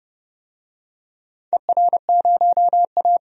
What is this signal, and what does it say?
Morse code, heard as a keyed tone, about 29 wpm, 700 Hz. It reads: ER0A
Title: DXCC-100 Call Signs - 4 Characters - 30wpm